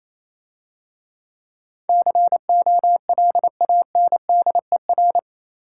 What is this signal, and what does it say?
Morse code, heard as a keyed tone, about 27 wpm, 690 Hz. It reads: COLANDER
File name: Easy Words - Speed-Racing with Hints - 22wpm